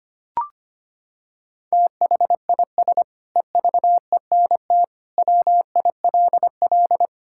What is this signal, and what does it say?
Morse code, heard as a keyed tone, about 25 wpm, 700 Hz. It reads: THIS EVENT WILL
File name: practice / 25wpm / Sentences with 9-Word Phrases - 25wpm